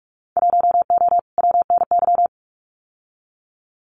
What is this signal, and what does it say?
Morse code, heard as a keyed tone, about 44 wpm, 700 Hz. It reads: 1O WNY
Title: Winter Field Day Exchanges Encore 45wpm